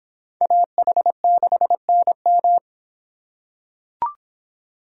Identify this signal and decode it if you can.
Morse code, heard as a keyed tone, about 26 wpm, 700 Hz. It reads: AH6NM